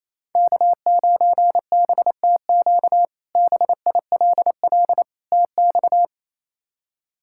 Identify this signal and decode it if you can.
Morse code, heard as a keyed tone, about 28 wpm, 700 Hz. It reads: K9BTQ BILL TX
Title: K1USN SST QSO Practice - 28wpm